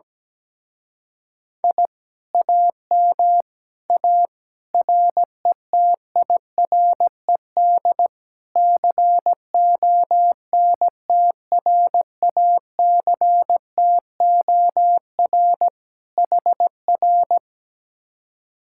Morse code, heard as a keyed tone, about 17 wpm, 700 Hz. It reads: I AM A RETIRED CONTRACTOR HR
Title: Speed-Racing Common QSO Phrases Encore - 17wpm